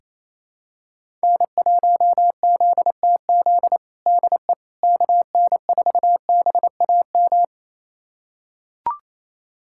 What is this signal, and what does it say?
Morse code, heard as a keyed tone, about 27 wpm, 700 Hz. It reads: N1ZTZ DE KN4BAM
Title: USA Callsign DE Callsign 28wpm